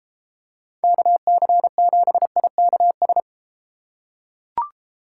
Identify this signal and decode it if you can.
Morse code, heard as a keyed tone, about 30 wpm, 700 Hz. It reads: KC7IKS